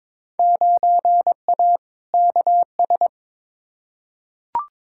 Morse code, heard as a keyed tone, about 22 wpm, 700 Hz. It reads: 9A KS
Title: Field Day Exchanges Encore - 22wpm